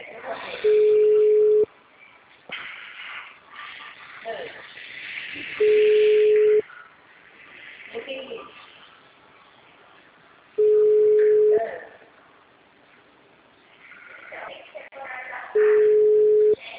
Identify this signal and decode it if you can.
Ringback tone (Europe, 425 Hz), with speech.
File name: out- -2017- - - .91567